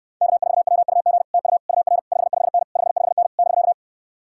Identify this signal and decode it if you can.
Morse code, heard as a keyed tone, about 69 wpm, 700 Hz. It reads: K4WUG TU UR 55N 55N <BK>